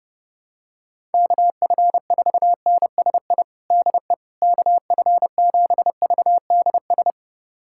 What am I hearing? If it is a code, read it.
Morse code, heard as a keyed tone, about 30 wpm, 700 Hz. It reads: KF4NSI DE KF7VDS
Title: USA Callsign DE Callsign - 6 Characters - 30wpm